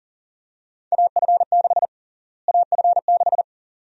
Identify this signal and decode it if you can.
Morse code, heard as a keyed tone, about 40 wpm, 700 Hz. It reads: AF6 AF6
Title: USA Call Sign Prefixes - Repeated 2x - 40wpm